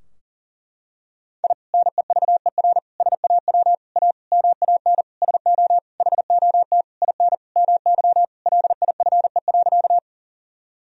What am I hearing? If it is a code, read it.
Morse code, heard as a keyed tone, about 37 wpm, 700 Hz. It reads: I NEVER SAW A MAN SO HOT IN MY LIFE.
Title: Sentences from Top 1000 Words - Review Part 3 - 40wpm